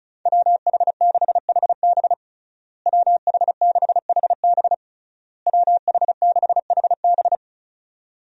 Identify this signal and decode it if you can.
Morse code, heard as a keyed tone, about 34 wpm, 700 Hz. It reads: WH6HB WH6HB WH6HB
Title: USA Call Signs - 5 Characters - Repeated 3x - 35wpm